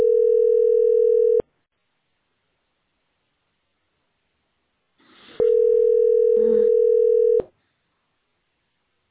Ringback tone (North America), 440 + 480 Hz.